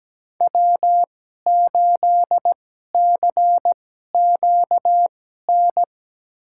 Morse code, heard as a keyed tone, about 17 wpm, spaced as for 12 wpm, 700 Hz. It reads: W8CQN
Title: USA Call Signs - 2x Character Spacing - 17wpm